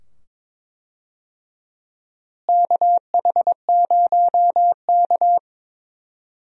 Morse code, heard as a keyed tone, about 22 wpm, 700 Hz. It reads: KH0K
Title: USA Call Signs - 4 Characters - Repeated 2x - 22wpm